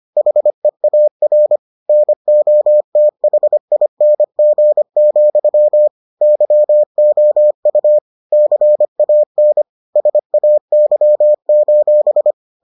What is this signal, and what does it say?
Morse code, heard as a keyed tone, about 25 wpm, 600 Hz. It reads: HEAR NOTHING, YOU CAN SAY: